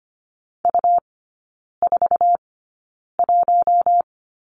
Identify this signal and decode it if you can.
Morse code, heard as a keyed tone, about 25 wpm, spaced as for 9 wpm, 700 Hz. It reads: U41
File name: Lesson 17e - Letter u - Three Characters 25wpm